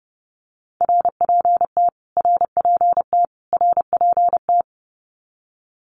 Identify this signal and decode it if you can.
Morse code, heard as a keyed tone, about 30 wpm, 700 Hz. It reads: RPT RPT RPT
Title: Abbreviations Repeated 3x - 30wpm